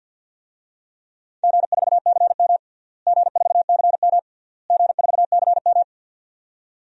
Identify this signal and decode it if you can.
Morse code, heard as a keyed tone, about 51 wpm, 700 Hz. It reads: G4CM G4CM G4CM
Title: DXCC-100 Call Signs - 4 Characters - Repeated 3x - 50wpm